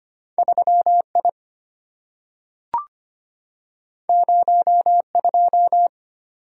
Morse code, heard as a keyed tone, about 25 wpm, 700 Hz. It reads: 3I 02
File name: Two Letter-Number Combinations - 25wpm